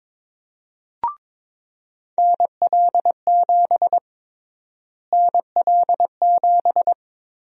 Morse code, heard as a keyed tone, about 21 wpm, 700 Hz. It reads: NL7 NL7
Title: USA Call Sign Prefixes - Repeated 2x - 22wpm